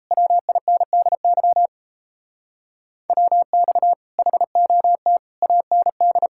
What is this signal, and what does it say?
Morse code, heard as a keyed tone, about 37 wpm, 700 Hz. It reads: WINDY WX HOT AND